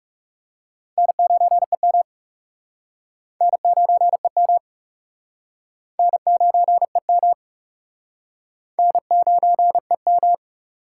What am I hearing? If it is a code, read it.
Morse code, heard as a keyed tone, about 41 wpm, 700 Hz. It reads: N9EM N9EM N9EM N9EM